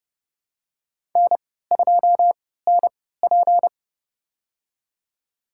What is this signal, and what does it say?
Morse code, heard as a keyed tone, about 30 wpm, spaced as for 17 wpm, 700 Hz. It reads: N2NP
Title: USA Call Signs - 4 Characters - 3x Character Spacing - 30wpm